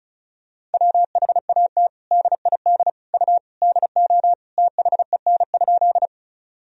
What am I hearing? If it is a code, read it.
Morse code, heard as a keyed tone, about 34 wpm, 700 Hz. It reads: WHAT DID U DO THEN?
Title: Sentences from Top 100 Words and Abbreviations 35wpm